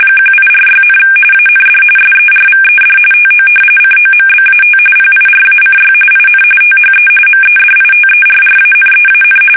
RTTY, about 45.45 baud. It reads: 1846, AND SERVED TO COMBAT PIRACY IN THE MEDITERRANEAN
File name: fsk850 / usnavy1